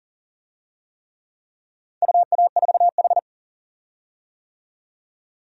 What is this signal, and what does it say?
Morse code, heard as a keyed tone, about 40 wpm, 700 Hz. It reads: UA4H
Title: DXCC-100 Call Signs - 4 Characters - 40wpm